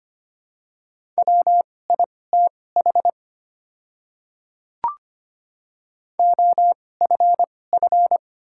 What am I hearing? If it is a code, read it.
Morse code, heard as a keyed tone, about 25 wpm, spaced as for 18 wpm, 700 Hz. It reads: WITH OFF
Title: 2x Character Spacing - Top 300 Words - 25wpm